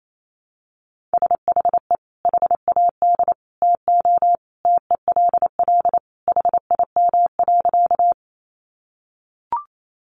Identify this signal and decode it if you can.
Morse code, heard as a keyed tone, about 28 wpm, 700 Hz. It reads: SHE HAD TO TELL HIM.
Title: Sentences from Top 200 Words 28wpm